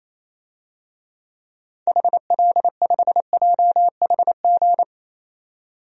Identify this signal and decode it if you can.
Morse code, heard as a keyed tone, about 28 wpm, 700 Hz. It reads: HL5JHG
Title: DXCC-100 Call Signs - 6 Characters - 28wpm